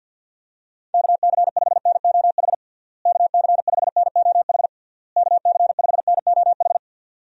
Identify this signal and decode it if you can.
Morse code, heard as a keyed tone, about 51 wpm, 700 Hz. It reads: KC5NOH KC5NOH KC5NOH
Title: USA Call Signs - 6 Characters - Repeated 3x - 50wpm